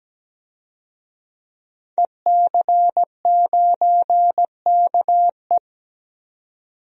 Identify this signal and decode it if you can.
Morse code, heard as a keyed tone, about 17 wpm, 700 Hz. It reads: EC9KE
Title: DXCC-100 Call Signs - 5 Characters - 17wpm